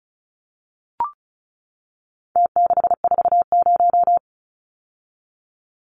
Morse code, heard as a keyed tone, about 34 wpm, 700 Hz. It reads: T640
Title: Numbers - Contest Style - 4-Digit 35wpm